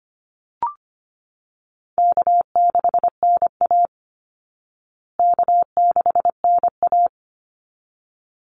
Morse code, heard as a keyed tone, about 25 wpm, 700 Hz. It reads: K6NA K6NA
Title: Easy Call Signs Repeated 2x - 25wpm